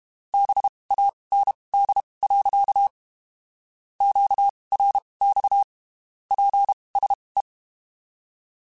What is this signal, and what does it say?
Morse code, heard as a keyed tone, about 32 wpm, spaced as for 23 wpm, 800 Hz. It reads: BAND. QRX PSE
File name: (spaced) / QSOs Spaced-mp3s / kc7haa @30 sp